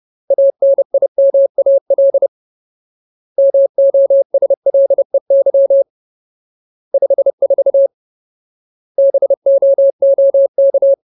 Morse code, heard as a keyed tone, about 30 wpm, 550 Hz. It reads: ANIMAL MOSLEY 54 BOOK
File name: Ultimate Word List / 30WPM Ultimate